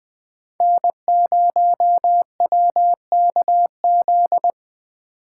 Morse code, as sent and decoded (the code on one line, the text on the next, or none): -. ----- .-- -.- --..
N0WKZ